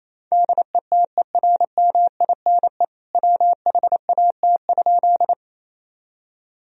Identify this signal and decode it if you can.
Morse code, heard as a keyed tone, about 28 wpm, 700 Hz. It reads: DETERMINE WHAT?
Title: Sentences from Top 1000 Words - Review Part 4 - 28wpm